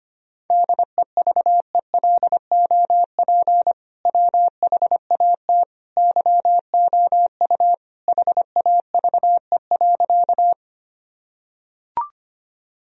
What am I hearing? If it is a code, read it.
Morse code, heard as a keyed tone, about 25 wpm, 700 Hz. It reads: DEVELOP WHAT YOU HAVE.
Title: Sentences from Top 1000 Words - Review Part 2 - 25wpm